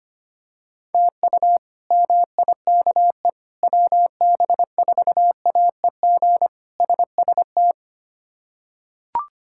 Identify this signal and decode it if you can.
Morse code, heard as a keyed tone, about 25 wpm, 700 Hz. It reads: TU MIKE WB4AEG SST
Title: K1USN SST QSO Practice - 25wpm